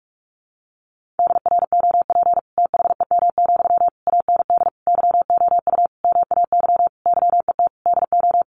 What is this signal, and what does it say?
Morse code, heard as a keyed tone, about 44 wpm, 700 Hz. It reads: DROP THEM, AND YOU MAY YET DO
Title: Sentences from Top 1400 Words 45wpm